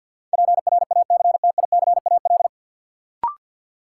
Morse code, heard as a keyed tone, about 53 wpm, 700 Hz. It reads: PRACTICAL
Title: Sentences from Top 1400 Words - New Word Drill Alt - 50wpm